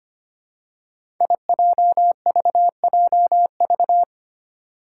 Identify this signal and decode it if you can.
Morse code, heard as a keyed tone, about 25 wpm, 700 Hz. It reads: IJVJV
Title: Five Challenging Letter-Number Combinations 25wpm